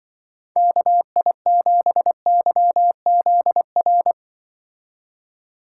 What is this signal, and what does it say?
Morse code, heard as a keyed tone, about 25 wpm, 700 Hz. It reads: KI7YZR
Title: Speed Racing - USA 2x3 Call Signs 20wpm